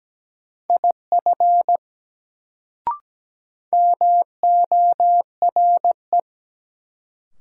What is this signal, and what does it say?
Morse code, heard as a keyed tone, about 17 wpm, 700 Hz. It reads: IF MORE
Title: Sentences from Top 100 Words - New Word Drill - 17wpm